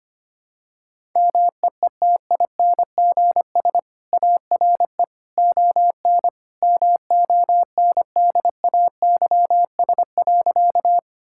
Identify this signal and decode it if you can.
Morse code, heard as a keyed tone, about 25 wpm, 700 Hz. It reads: MEETINGS ARE ON MONDAYS.
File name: Easy Sentences with Hints - 25wpm